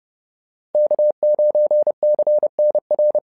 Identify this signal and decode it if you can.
Morse code, heard as a keyed tone, about 30 wpm, 600 Hz. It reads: K9CNR
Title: SOTA QSO Practice 30wpm